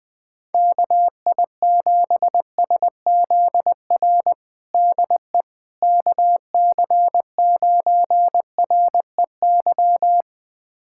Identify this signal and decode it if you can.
Morse code, heard as a keyed tone, about 21 wpm, 700 Hz. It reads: KI7SZR DE KC9REY
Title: USA Callsign DE Callsign - 6 Characters - 20wpm